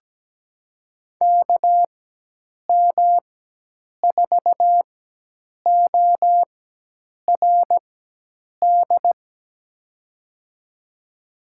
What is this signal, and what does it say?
Morse code, heard as a keyed tone, about 17 wpm, spaced as for 8 wpm, 700 Hz. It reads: KM4ORD
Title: USA Call Signs - 6 Characters - 4x Character Spacing - 17wpm